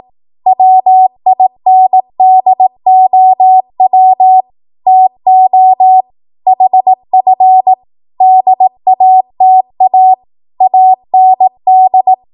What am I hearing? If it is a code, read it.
Morse code, heard as a keyed tone, about 18 wpm, 750 Hz. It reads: WINDOW TO HF DATA AND
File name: Morse / 18WPM